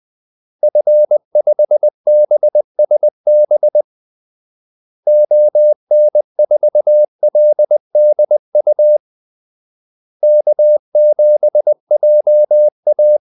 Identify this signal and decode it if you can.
Morse code, heard as a keyed tone, about 20 wpm, 600 Hz. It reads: F5BSB ON4LDU K7JA
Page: Kaldesignaler 20 wpm